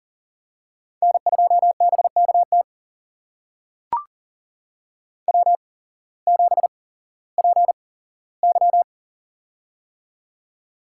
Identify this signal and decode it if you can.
Morse code, heard as a keyed tone, about 40 wpm, spaced as for 19 wpm, 700 Hz. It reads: N2BKT W7PY
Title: USA Call Signs - 8x Character Spacing - 40wpm